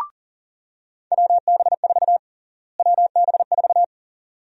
Morse code, heard as a keyed tone, about 38 wpm, 700 Hz. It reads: WB4 WB4
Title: USA Call Sign Prefixes - Repeated 2x - 40wpm